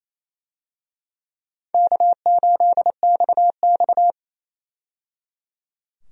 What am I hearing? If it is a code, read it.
Morse code, heard as a keyed tone, about 28 wpm, 700 Hz. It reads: K8XX
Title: USA Call Signs - 4 Characters - 28wpm